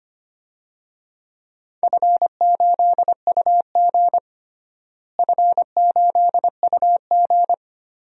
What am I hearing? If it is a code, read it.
Morse code, heard as a keyed tone, about 25 wpm, 700 Hz. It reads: F8UG F8UG
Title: DXCC-100 Call Signs - 4 Characters - Repeated 2x - 25wpm